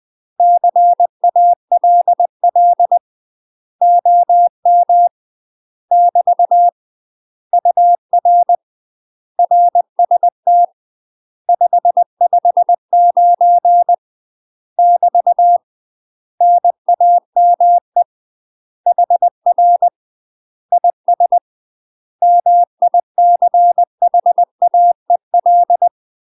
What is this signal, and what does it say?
Morse code, heard as a keyed tone, about 20 wpm, 700 Hz. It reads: CALL OM <BT> UR RST 559 <BT> NAME HR IS MICHAEL
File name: UK MASTER QSOs TRAINING / 20WPM / UK MASTER QSOs 0008